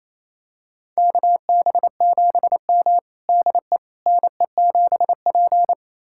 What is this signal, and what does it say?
Morse code, heard as a keyed tone, about 28 wpm, 700 Hz. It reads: KB7M DE NE7P
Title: USA Callsign DE Callsign - 4 Characters - 28wpm